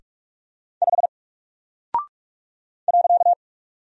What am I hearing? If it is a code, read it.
Morse code, heard as a keyed tone, about 44 wpm, 700 Hz. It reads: 5 .
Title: Single Character (Letter:Number:Punctuation:Prosign) Alt 45wpm